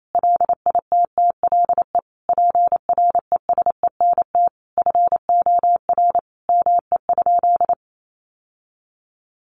Morse code, heard as a keyed tone, about 28 wpm, 700 Hz. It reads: LITTLE PRESENT FOR ME?
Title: Sentences from Top 800 Words 28wpm